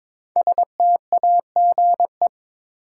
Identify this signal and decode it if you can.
Morse code, heard as a keyed tone, about 22 wpm, 700 Hz. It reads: STAGE
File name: 5 Letter Words - First 3 Letters Given - 22wpm